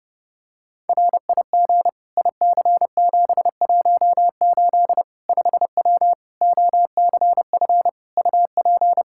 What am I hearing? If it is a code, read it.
Morse code, heard as a keyed tone, about 30 wpm, 700 Hz. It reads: RIG IC718 5W OCF UP
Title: K1USN SST QSO Practice 30wpm